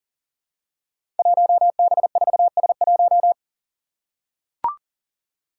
Morse code, heard as a keyed tone, about 40 wpm, 700 Hz. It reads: 1B4S1
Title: Five Challenging Letter-Number Combinations 40wpm